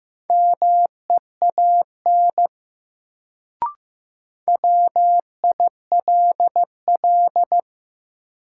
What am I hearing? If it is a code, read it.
Morse code, heard as a keyed tone, about 15 wpm, 700 Hz. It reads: MEAN WILL
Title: Sets of 2 Words - New Word Drill - 15wpm